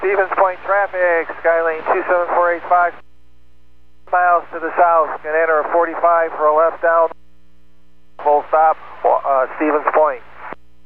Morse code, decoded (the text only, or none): none